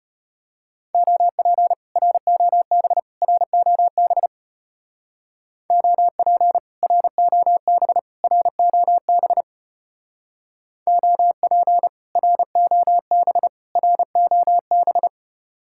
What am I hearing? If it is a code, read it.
Morse code, heard as a keyed tone, about 31 wpm, 700 Hz. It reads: OP ROB ROB OP ROB ROB OP ROB ROB
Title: Speed-Racing Common QSO Phrases Encore - 25wpm